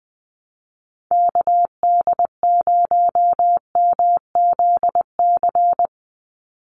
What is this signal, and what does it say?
Morse code, heard as a keyed tone, about 20 wpm, 700 Hz. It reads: KD0MZC
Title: USA Call Signs 20wpm